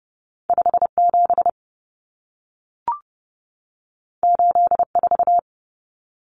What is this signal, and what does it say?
Morse code, heard as a keyed tone, about 30 wpm, 700 Hz. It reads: 57 84
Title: Common Cut Numbers - 2-Digit 30wpm